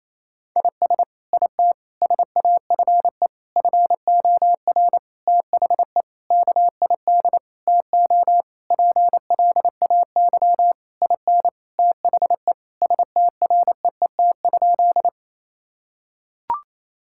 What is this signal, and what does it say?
Morse code, heard as a keyed tone, about 27 wpm, 700 Hz. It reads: IS IT SAFE FOR THE KID TO PLAY IN THE STREET?
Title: Sentences from Top 1600 Words 28wpm